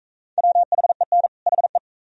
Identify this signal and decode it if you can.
Morse code, heard as a keyed tone, about 42 wpm, 700 Hz. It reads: WHEN HE